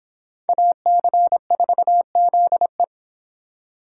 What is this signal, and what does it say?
Morse code, heard as a keyed tone, about 25 wpm, 700 Hz. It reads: AC4ZE